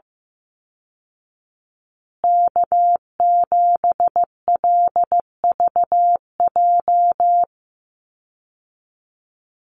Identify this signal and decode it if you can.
Morse code, heard as a keyed tone, about 15 wpm, 700 Hz. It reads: K7LVJ
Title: Call Signs Repeated 2x - 15wpm